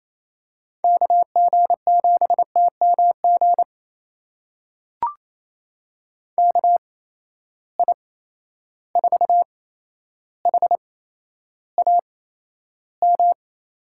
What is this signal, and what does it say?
Morse code, heard as a keyed tone, about 28 wpm, spaced as for 12 wpm, 700 Hz. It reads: KG7TMG KI4HAM
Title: USA Call Signs - 6 Characters - 8x Character Spacing - 28wpm